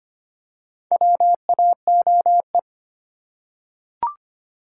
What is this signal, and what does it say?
Morse code, heard as a keyed tone, about 25 wpm, 700 Hz. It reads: WAOE
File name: practice / 25wpm / Four Letter-Number Combinations - 25wpm